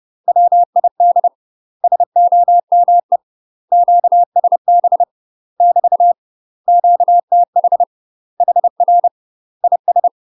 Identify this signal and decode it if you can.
Morse code, heard as a keyed tone, about 30 wpm, 700 Hz. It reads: WID SOME QSB <BT> QTH HR IS
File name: UK MASTER QSOs TRAINING / 30WPM / UK MASTER QSOs 0016